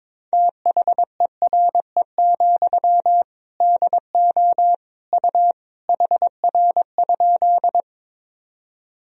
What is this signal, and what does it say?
Morse code, heard as a keyed tone, about 21 wpm, 700 Hz. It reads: THERE, DO U HR?
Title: Sentences from Top 300 Words and Abbreviations 22wpm